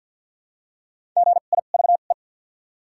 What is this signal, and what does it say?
Morse code, heard as a keyed tone, about 53 wpm, spaced as for 37 wpm, 700 Hz. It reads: GIVE
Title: 2x Character Spacing - Top 500 Words - 50wpm